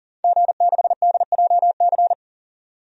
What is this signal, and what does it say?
Morse code, heard as a keyed tone, about 40 wpm, 700 Hz. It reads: G6DJC
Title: DXCC-100 Call Signs - 5 Characters - 40wpm